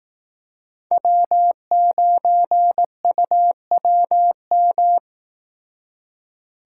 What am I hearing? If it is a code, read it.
Morse code, heard as a keyed tone, about 18 wpm, 700 Hz. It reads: W9UWM
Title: Speed Racing - USA 1x3 Call Signs 15wpm